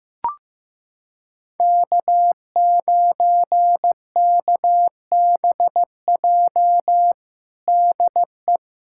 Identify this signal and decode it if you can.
Morse code, heard as a keyed tone, about 15 wpm, 700 Hz. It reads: K9KBJ DE
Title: USA Callsign DE Callsign - 5 Characters - 15wpm